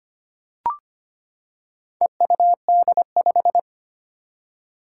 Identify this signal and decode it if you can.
Morse code, heard as a keyed tone, about 25 wpm, 700 Hz. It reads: EUD5
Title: Four Challenging Letter-Number Combinations 25wpm